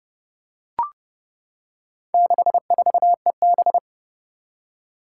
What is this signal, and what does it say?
Morse code, heard as a keyed tone, about 30 wpm, 700 Hz. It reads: 64EB
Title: Four Challenging Letter-Number Combinations 30wpm